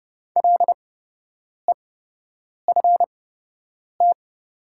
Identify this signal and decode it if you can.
Morse code, heard as a keyed tone, about 30 wpm, spaced as for 8 wpm, 700 Hz. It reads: LEFT
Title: 8x Character Spacing - Top 300 Words - 30wpm